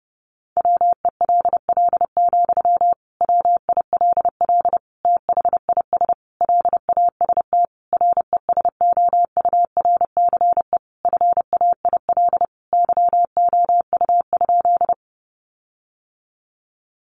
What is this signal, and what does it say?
Morse code, heard as a keyed tone, about 30 wpm, 700 Hz. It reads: WELL, WILL THIS LAST RESOURCE FAIL YOU?
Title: Sentences from Top 1300 Words 30wpm